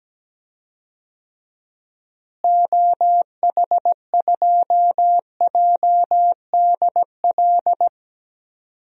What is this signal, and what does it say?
Morse code, heard as a keyed tone, about 17 wpm, 700 Hz. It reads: OH2JDL
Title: DXCC-100 Call Signs - 6 Characters - Repeated 2x - 17wpm